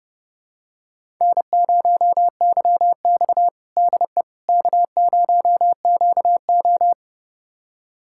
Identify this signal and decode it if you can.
Morse code, heard as a keyed tone, about 30 wpm, 700 Hz. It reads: N0YX DE K0QO
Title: USA Callsign DE Callsign - 4 Characters - 30wpm